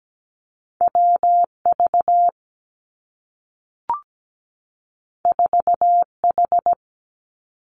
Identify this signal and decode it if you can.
Morse code, heard as a keyed tone, about 17 wpm, 700 Hz. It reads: W V 4 H
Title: Two Challenging Letter-Number Combinations 17wpm